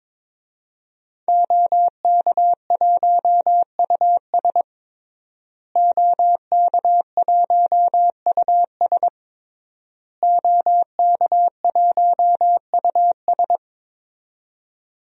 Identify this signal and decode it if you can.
Morse code, heard as a keyed tone, about 22 wpm, 700 Hz. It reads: OK1US OK1US OK1US
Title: DXCC-100 Call Signs - 5 Characters - Repeated 3x - 22wpm